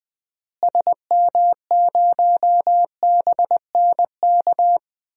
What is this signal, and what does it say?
Morse code, heard as a keyed tone, about 20 wpm, 700 Hz. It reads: SM0BNK
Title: DXCC-100 Call Signs - 6 Characters - Repeated 3x - 20wpm